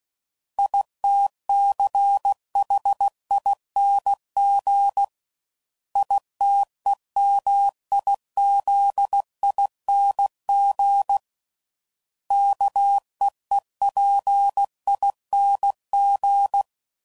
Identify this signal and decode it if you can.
Morse code, heard as a keyed tone, about 16 wpm, 800 Hz. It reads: ITCHING ITEMIZING KEEPING
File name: Suffixes / Special INGs / Sngl Ings